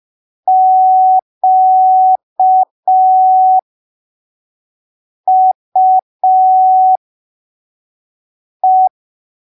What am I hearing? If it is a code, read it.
Morse code, heard as a keyed tone, about 5 wpm, spaced as for 3 wpm, 750 Hz. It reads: QUE
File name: arquivos / texto / texto 08 05ppm